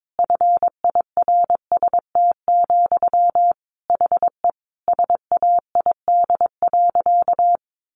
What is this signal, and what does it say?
Morse code, heard as a keyed tone, about 22 wpm, 700 Hz. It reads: FIRST, HE SAID.
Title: Sentences from Top 1300 Words 22wpm